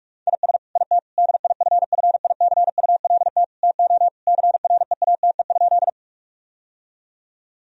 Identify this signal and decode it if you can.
Morse code, heard as a keyed tone, about 46 wpm, 700 Hz. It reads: IS IT DIFFICULT TO CREATE?
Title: Sentences from Top 1000 Words - Review Part 3 - 45wpm